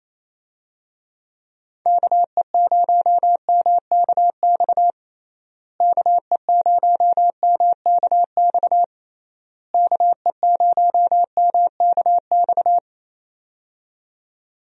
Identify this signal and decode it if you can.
Morse code, heard as a keyed tone, about 28 wpm, 700 Hz. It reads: KE0MKX KE0MKX KE0MKX
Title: USA Call Signs - 6 Characters - Repeated 3x - 28wpm